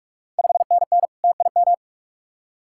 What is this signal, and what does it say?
Morse code, heard as a keyed tone, about 46 wpm, 700 Hz. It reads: 5NN TIM
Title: Ontario QSO Party Exchange 45wpm